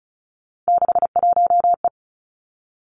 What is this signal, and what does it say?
Morse code, heard as a keyed tone, about 36 wpm, 700 Hz. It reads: 61E
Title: Three Challenging Letter-Number Combinations 35wpm